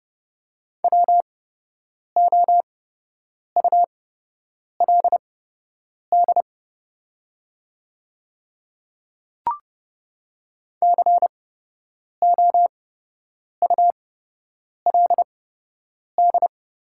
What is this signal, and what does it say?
Morse code, heard as a keyed tone, about 30 wpm, spaced as for 8 wpm, 700 Hz. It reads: WOULD COULD
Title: 8x Character Spacing - Top 100 Words - 30wpm